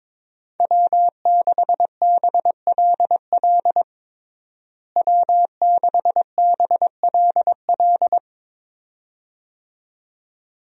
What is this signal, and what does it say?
Morse code, heard as a keyed tone, about 22 wpm, 700 Hz. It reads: W6BLL W6BLL
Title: USA Call Signs - 5 Characters - Repeated 2x - 22wpm